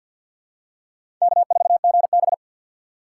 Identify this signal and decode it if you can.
Morse code, heard as a keyed tone, about 52 wpm, 700 Hz. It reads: K4GB
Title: USA Call Signs - 4 Characters - 2x Character Spacing - 50wpm